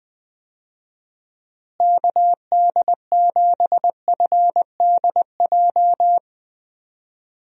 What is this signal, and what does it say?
Morse code, heard as a keyed tone, about 20 wpm, 700 Hz. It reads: KD7FDJ
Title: USA Call Signs - 6 Characters - Repeated 3x - 20wpm